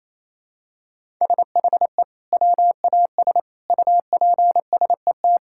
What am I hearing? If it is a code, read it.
Morse code, heard as a keyed tone, about 28 wpm, 700 Hz. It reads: SHE WAS UPSET
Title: Sentences from Top 1000 Words - Review Part 5 - 28wpm